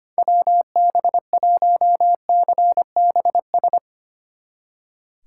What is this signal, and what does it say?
Morse code, heard as a keyed tone, about 32 wpm, 700 Hz. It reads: WB1CBS